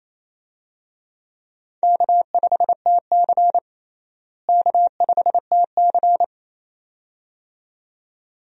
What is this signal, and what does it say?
Morse code, heard as a keyed tone, about 28 wpm, 700 Hz. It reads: K5TC K5TC
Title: USA Call Signs - 4 Characters - Repeated 2x - 28wpm